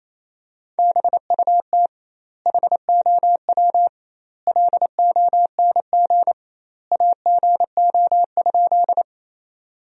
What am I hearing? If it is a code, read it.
Morse code, heard as a keyed tone, about 28 wpm, 700 Hz. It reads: BUT HOW LONG AGO?
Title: 2x Word-Spacing - Sentences from Top 500 Words - 28wpm